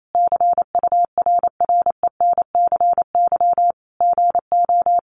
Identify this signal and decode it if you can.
Morse code, heard as a keyed tone, about 28 wpm, 700 Hz. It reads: CURRENCY GO
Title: Sentences from Top 1500 Words 28wpm